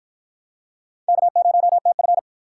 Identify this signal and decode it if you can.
Morse code, heard as a keyed tone, about 48 wpm, 700 Hz. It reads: K0TF